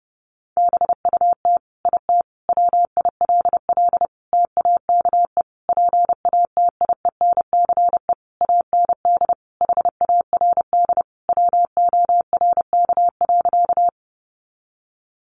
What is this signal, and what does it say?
Morse code, heard as a keyed tone, about 30 wpm, 700 Hz. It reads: BUT IT WILL TAKE PATIENCE AND HARD WORK.
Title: Sentences from Top 1300 Words 30wpm